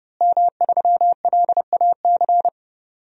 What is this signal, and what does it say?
Morse code, heard as a keyed tone, about 30 wpm, 700 Hz. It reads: M3LAC
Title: DXCC-100 Call Signs - 5 Characters - Repeated 3x - 30wpm